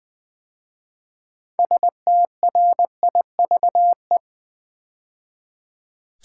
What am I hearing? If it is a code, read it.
Morse code, heard as a keyed tone, about 20 wpm, 700 Hz. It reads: STRIVE
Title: 6 Letter Words - First 3 Letters Given - 20wpm